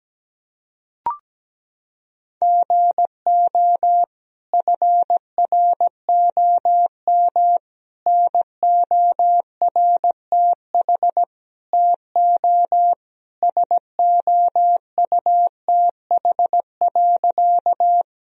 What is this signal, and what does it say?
Morse code, heard as a keyed tone, about 17 wpm, 700 Hz. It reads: GO FROM NORTH TO SOUTH.
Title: Sentences from Top 1000 Words - Review Part 2 - 17wpm